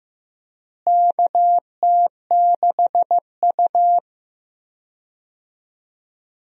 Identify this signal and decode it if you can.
Morse code, heard as a keyed tone, about 15 wpm, 700 Hz. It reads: KT6U
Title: USA Call Signs - 4 Characters - Repeated 2x - 15wpm